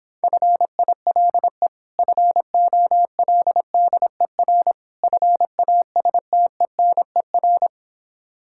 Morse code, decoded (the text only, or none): FILE FOLDER FASTENER